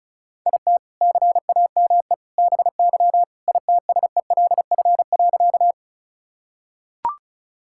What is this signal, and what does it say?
Morse code, heard as a keyed tone, about 36 wpm, 700 Hz. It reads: IT CAME BY ITSELF.
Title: Sentences from Top 1000 Words - Review Part 5 - 35wpm